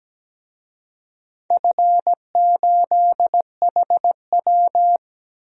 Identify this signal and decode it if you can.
Morse code, heard as a keyed tone, about 17 wpm, 700 Hz. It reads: F8HW
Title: DXCC-100 Call Signs - 4 Characters - Repeated 3x - 17wpm